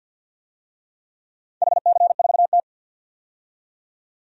Morse code, heard as a keyed tone, about 51 wpm, 700 Hz. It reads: HC4T
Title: DXCC-100 Call Signs - 4 Characters - 50wpm